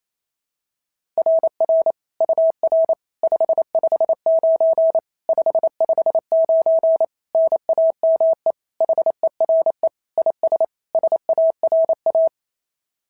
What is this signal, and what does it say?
Morse code, heard as a keyed tone, about 27 wpm, 650 Hz. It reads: RR UR 559 559 NAME HERE IS SARA
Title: SOTA QSO Practice 28wpm